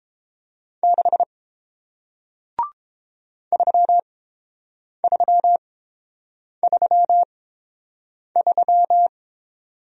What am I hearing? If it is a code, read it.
Morse code, heard as a keyed tone, about 30 wpm, spaced as for 7 wpm, 700 Hz. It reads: 6 3333